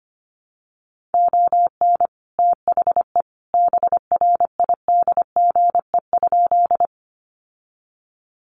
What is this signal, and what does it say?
Morse code, heard as a keyed tone, about 25 wpm, 700 Hz. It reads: ON THE BRIDGE?
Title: Sentences from Top 1200 Words 25wpm